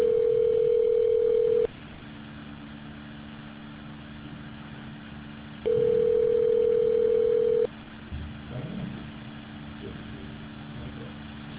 Ringback tone (North America), 440 + 480 Hz.